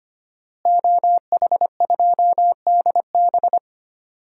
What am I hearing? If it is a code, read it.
Morse code, heard as a keyed tone, about 25 wpm, 700 Hz. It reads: OH2DB
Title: DXCC-100 Call Signs - 5 Characters - Repeated 2x - 25wpm